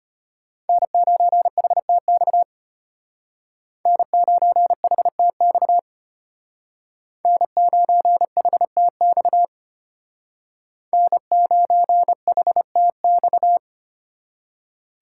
Morse code, heard as a keyed tone, about 33 wpm, 700 Hz. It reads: N9HTX N9HTX N9HTX N9HTX